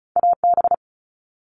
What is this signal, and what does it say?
Morse code, heard as a keyed tone, about 36 wpm, 700 Hz. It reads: AB
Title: Canadian Provinces and Territories - Repeated 3x - 35wpm